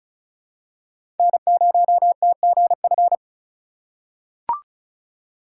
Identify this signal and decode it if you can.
Morse code, heard as a keyed tone, about 36 wpm, 700 Hz. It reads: N0TGF
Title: USA Call Signs - 5 Characters - 3x Character Spacing - 35wpm